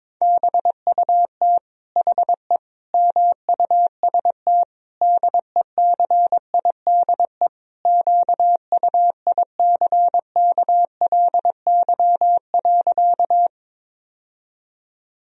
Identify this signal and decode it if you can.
Morse code, heard as a keyed tone, about 22 wpm, 700 Hz. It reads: BUT HE MUST DECIDE QUICKLY.
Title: Sentences from Top 1000 Words - Review Part 3 - 22wpm